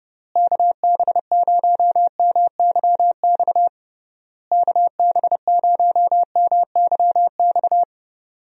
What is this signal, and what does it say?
Morse code, heard as a keyed tone, about 30 wpm, 700 Hz. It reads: KB0MYX KB0MYX
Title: USA Call Signs - 6 Characters - Repeated 3x - 30wpm